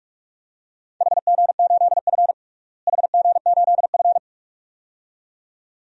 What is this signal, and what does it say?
Morse code, heard as a keyed tone, about 46 wpm, 700 Hz. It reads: HG8F HG8F
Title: DXCC-100 Call Signs - 4 Characters - Repeated 2x - 45wpm